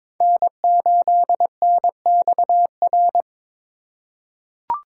Morse code, heard as a keyed tone, about 22 wpm, 700 Hz. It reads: N8NXR
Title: USA Call Signs - 5 Characters - 4x Character Spacing - 22wpm